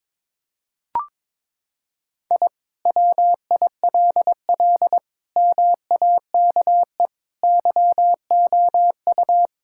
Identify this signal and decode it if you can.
Morse code, heard as a keyed tone, about 22 wpm, 700 Hz. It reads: I WILL MAKE YOU
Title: Sentences from Top 1000 Words - Review Part 5 - 22wpm